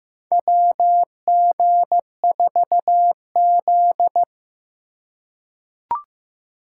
Morse code, heard as a keyed tone, about 15 wpm, 700 Hz. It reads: WG4Z
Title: USA Call Signs - 3x Character Spacing - 15wpm